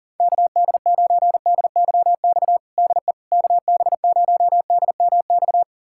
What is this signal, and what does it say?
Morse code, heard as a keyed tone, about 37 wpm, 700 Hz. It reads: KD9DYX DE KB0DMX
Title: USA Callsign DE Callsign 40wpm